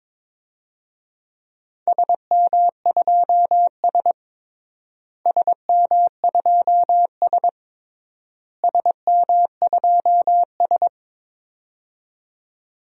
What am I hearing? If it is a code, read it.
Morse code, heard as a keyed tone, about 22 wpm, 700 Hz. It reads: SM2S SM2S SM2S
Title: DXCC-100 Call Signs - 4 Characters - Repeated 3x - 22wpm